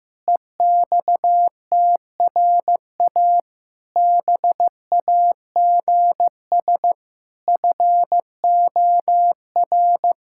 Morse code, heard as a keyed tone, about 16 wpm, 700 Hz. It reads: EXTRA BAGS FOR
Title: Pangram Challenge v2 - 15wpm